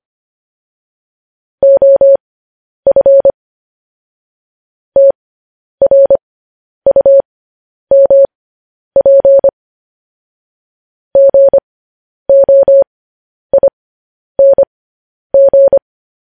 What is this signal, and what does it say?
Morse code, heard as a keyed tone, about 25 wpm, spaced as for 10 wpm, 550 Hz. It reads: OF TRUMP GOING
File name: podcast / News headlines-1x-25x10